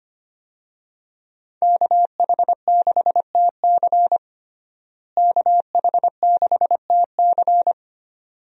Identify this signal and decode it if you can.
Morse code, heard as a keyed tone, about 25 wpm, 700 Hz. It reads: KH6TC KH6TC
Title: DXCC-100 Call Signs - 5 Characters - Repeated 2x - 25wpm